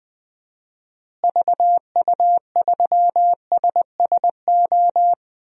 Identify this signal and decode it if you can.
Morse code, heard as a keyed tone, about 21 wpm, 700 Hz. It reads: VU3SSO
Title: DXCC-100 Call Signs - 6 Characters - Repeated 2x - 20wpm